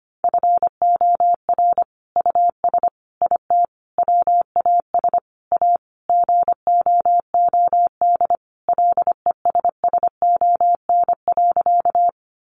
Morse code, heard as a keyed tone, about 25 wpm, 700 Hz. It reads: FOR US IT WAS A GOOD LESSON.
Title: Sentences from Top 1200 Words 25wpm